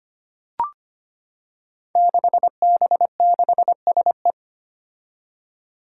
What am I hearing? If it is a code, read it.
Morse code, heard as a keyed tone, about 25 wpm, 700 Hz. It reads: 6B6SE
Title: Five Challenging Letter-Number Combinations 25wpm